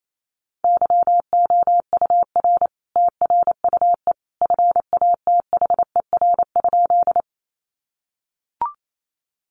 Morse code, heard as a keyed tone, about 28 wpm, 700 Hz. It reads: YOUR TRUE FATHER?
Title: Sentences from Top 400 Words 28wpm